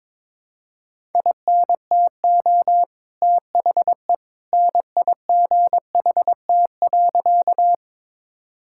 Morse code, heard as a keyed tone, about 22 wpm, 700 Hz. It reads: INTO THE NIGHT.
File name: practice / 22wpm / Sentences with 2-Word Phrases - 22wpm